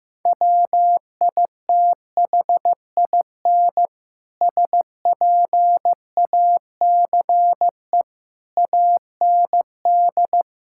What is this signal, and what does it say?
Morse code, heard as a keyed tone, about 15 wpm, 700 Hz. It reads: WITHIN SPACE AND
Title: Sentences from Top 1000 Words - Review Part 5 - 15wpm